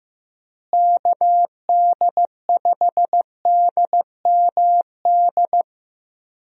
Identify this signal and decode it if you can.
Morse code, heard as a keyed tone, about 16 wpm, 700 Hz. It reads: KD5DMD
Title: USA Call Signs - 6 Characters - Repeated 2x - 15wpm